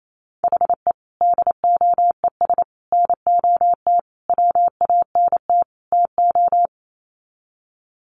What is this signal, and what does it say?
Morse code, heard as a keyed tone, about 28 wpm, 700 Hz. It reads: HE DOES NOT WANT TO
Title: Sets of 5 Words 28wpm